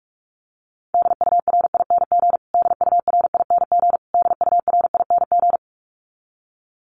Morse code, heard as a keyed tone, about 44 wpm, 700 Hz. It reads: DURING DURING DURING
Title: Top 500 Words Repeated 45wpm